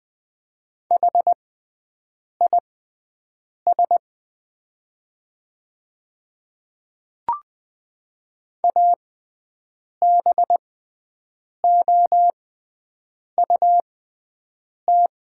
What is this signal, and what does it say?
Morse code, heard as a keyed tone, about 20 wpm, spaced as for 7 wpm, 700 Hz. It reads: HIS ABOUT
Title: 6x Character Spacing - Top 100 Words - 20wpm